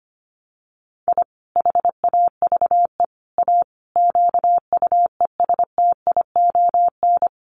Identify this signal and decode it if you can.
Morse code, heard as a keyed tone, about 25 wpm, 700 Hz. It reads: I HAVE A QUESTION
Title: Sets of 5 Words Encore2 25wpm